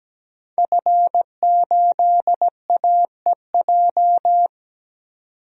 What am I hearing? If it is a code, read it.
Morse code, heard as a keyed tone, about 17 wpm, 700 Hz. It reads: F8AEJ
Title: DXCC-100 Call Signs - 5 Characters - 17wpm